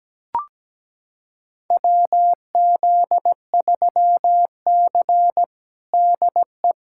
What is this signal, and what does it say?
Morse code, heard as a keyed tone, about 17 wpm, 700 Hz. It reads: WZ3C DE
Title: USA Callsign DE Callsign - 4 Characters - 17wpm